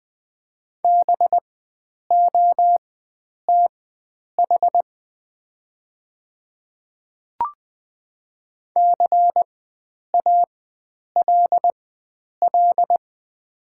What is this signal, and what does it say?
Morse code, heard as a keyed tone, about 20 wpm, spaced as for 9 wpm, 690 Hz. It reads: BOTH CALL
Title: 4x Character Spacing - Top 300 Words - 20wpm